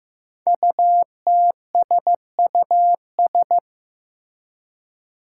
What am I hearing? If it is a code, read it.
Morse code, heard as a keyed tone, about 15 wpm, 700 Hz. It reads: UTSUS
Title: Five Challenging Letter-Number Combinations 15wpm